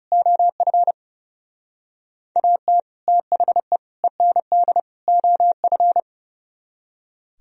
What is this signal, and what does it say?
Morse code, heard as a keyed tone, about 35 wpm, 700 Hz. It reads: OF AT THE END OF
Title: Speed-Racing - Sets of 4 Words 30wpm